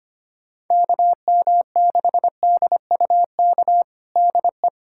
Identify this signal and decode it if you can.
Morse code, heard as a keyed tone, about 25 wpm, 700 Hz. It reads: KM6DUK DE
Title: USA Callsign DE Callsign - 6 Characters - 25wpm